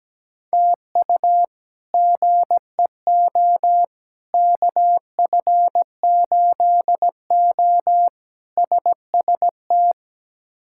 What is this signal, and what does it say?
Morse code, heard as a keyed tone, about 17 wpm, 700 Hz. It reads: TU GEO KF8O SST
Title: K1USN SST QSO Practice - 17wpm